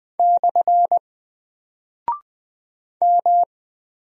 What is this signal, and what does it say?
Morse code, heard as a keyed tone, about 20 wpm, 700 Hz. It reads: / M
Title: Single Character (Letter:Number:Punctuation:Prosign) 20wpm